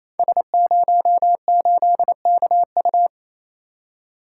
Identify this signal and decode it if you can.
Morse code, heard as a keyed tone, about 27 wpm, 700 Hz. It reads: S08KU
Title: Code Groups - 5 Letter:Numbers - 28wpm